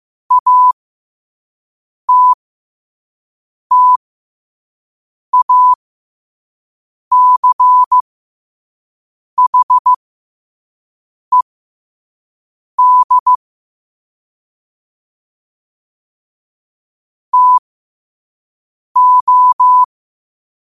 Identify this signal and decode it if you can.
Morse code, heard as a keyed tone, about 15 wpm, spaced as for 5 wpm, 1000 Hz. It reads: ATTACHED TO
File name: qotdfiles / 5wpm